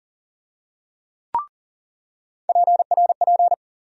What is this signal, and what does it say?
Morse code, heard as a keyed tone, about 40 wpm, 700 Hz. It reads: PRP
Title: USA Call Sign Postfixes - 3 Letters 40wpm